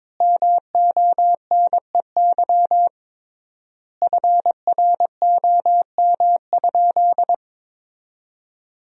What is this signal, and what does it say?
Morse code, heard as a keyed tone, about 22 wpm, 700 Hz. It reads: MONEY FROM?
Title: 3x Word-Spacing - Sentences from Top 400 Words - 22wpm